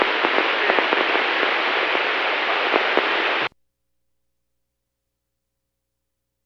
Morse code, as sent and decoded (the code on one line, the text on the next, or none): none